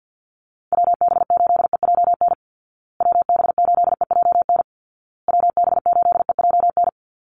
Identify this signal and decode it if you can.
Morse code, heard as a keyed tone, about 50 wpm, 700 Hz. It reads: WB8EJN WB8EJN WB8EJN
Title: Call Signs Repeated 2x - 50wpm